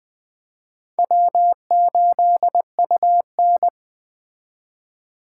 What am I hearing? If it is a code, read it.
Morse code, heard as a keyed tone, about 21 wpm, 700 Hz. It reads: W8UN